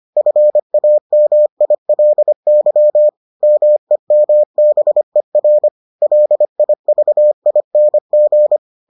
Morse code, heard as a keyed tone, about 25 wpm, 600 Hz. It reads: FAMILY MEMBER LIVING